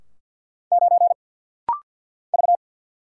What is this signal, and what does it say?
Morse code, heard as a keyed tone, about 49 wpm, 700 Hz. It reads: 9 V
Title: Single Letter-Number - Rapid-Fire 50wpm